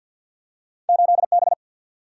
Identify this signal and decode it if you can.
Morse code, heard as a keyed tone, about 53 wpm, 700 Hz. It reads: 8 B
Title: Single-Letter Single-Number 50wpm